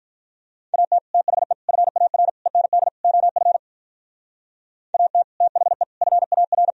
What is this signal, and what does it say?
Morse code, heard as a keyed tone, about 52 wpm, 700 Hz. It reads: AT THE FAR END OF AT THE FAR